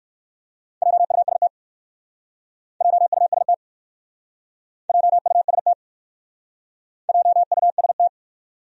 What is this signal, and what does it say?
Morse code, heard as a keyed tone, about 56 wpm, 700 Hz. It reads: JUST JUST JUST JUST
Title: Speed-Racing Top 200 Words 45wpm